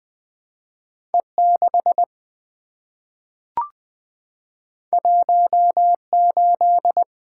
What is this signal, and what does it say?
Morse code, heard as a keyed tone, about 20 wpm, 700 Hz. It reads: E6 18
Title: Two Letter-Number Combinations - 20wpm